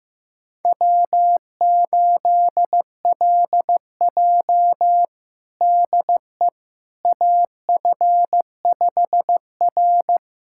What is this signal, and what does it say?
Morse code, heard as a keyed tone, about 15 wpm, 700 Hz. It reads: W8LJ DE AF5R
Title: USA Callsign DE Callsign - 4 Characters - 15wpm